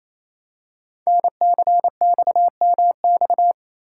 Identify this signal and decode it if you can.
Morse code, heard as a keyed tone, about 28 wpm, 700 Hz. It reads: NCXMX
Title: Code Groups - 5 Letters - 28wpm